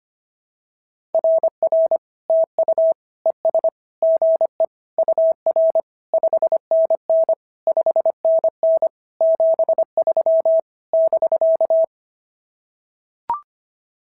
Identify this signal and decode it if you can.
Morse code, heard as a keyed tone, about 25 wpm, 650 Hz. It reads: RR TU ES GE UR 5NN 5NN 73 <BK>
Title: SOTA QSO Practice 25wpm